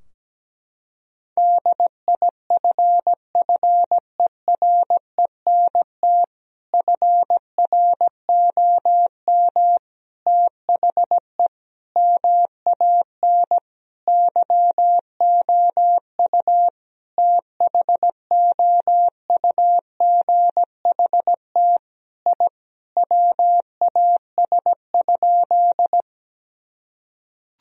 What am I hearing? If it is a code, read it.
Morse code, heard as a keyed tone, about 17 wpm, 700 Hz. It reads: DIFFERENT FROM THE MAN YOU THOUGHT I WAS?
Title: Sentences from Top 1000 Words - Review Part 3 - 17wpm